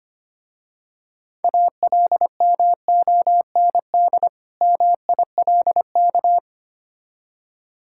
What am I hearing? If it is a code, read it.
Morse code, heard as a keyed tone, about 25 wpm, 700 Hz. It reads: ALMOND MILK
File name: practice / 25wpm / Difficult Words with Hints - 25wpm